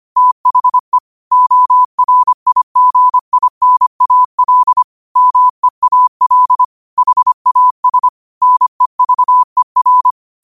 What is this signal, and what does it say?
Morse code, heard as a keyed tone, about 25 wpm, 1000 Hz. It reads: THE ORIGINAL MEAL HAS NEVER
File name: qotdfiles / 25wpm